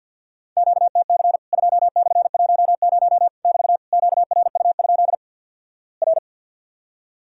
Morse code, heard as a keyed tone, about 50 wpm, 700 Hz. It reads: QTC 2/10 <BT> QRU? R
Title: Worked All Europe DX Contest - QTC Exchange with Repeats - 50wpm